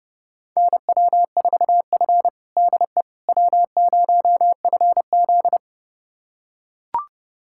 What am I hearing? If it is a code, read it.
Morse code, heard as a keyed tone, about 30 wpm, 700 Hz. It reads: NW4F DE W0FZ
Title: USA Callsign DE Callsign - 4 Characters - 30wpm